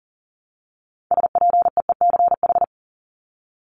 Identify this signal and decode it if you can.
Morse code, heard as a keyed tone, about 40 wpm, 700 Hz. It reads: SPEECH
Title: Top 1751-2000 Words 40wpm